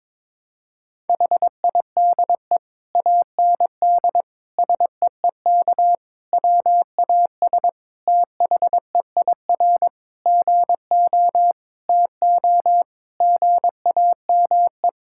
Morse code, heard as a keyed tone, about 22 wpm, 700 Hz. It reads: HIDE AND SEEK WAS THEIR GO TO GAME
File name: Difficult Sentences with Hints - 22wpm